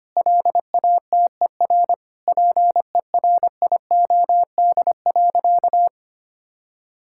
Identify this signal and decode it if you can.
Morse code, heard as a keyed tone, about 25 wpm, 700 Hz. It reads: LATER PERIOD.
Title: Sentences from Top 1000 Words - Review Part 5 - 25wpm